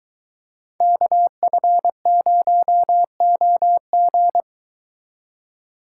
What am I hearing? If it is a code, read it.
Morse code, heard as a keyed tone, about 22 wpm, 700 Hz. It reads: KF0OG